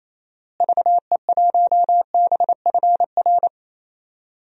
Morse code, heard as a keyed tone, about 28 wpm, 700 Hz. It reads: VE1BFR
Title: DXCC-100 Call Signs - 6 Characters - Repeated 2x - 28wpm